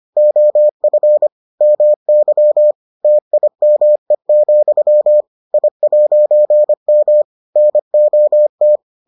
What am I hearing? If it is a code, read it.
Morse code, heard as a keyed tone, about 25 wpm, 600 Hz. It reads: OF MY TIME, I'M NOT